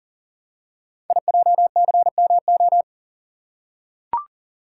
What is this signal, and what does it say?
Morse code, heard as a keyed tone, about 40 wpm, 700 Hz. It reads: IJCMO
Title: Code Groups - 5 Letters - 40wpm